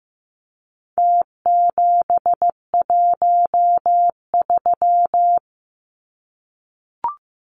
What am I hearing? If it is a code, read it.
Morse code, heard as a keyed tone, about 15 wpm, 700 Hz. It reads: T713
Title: Numbers - Contest Style - 4-Digit 15wpm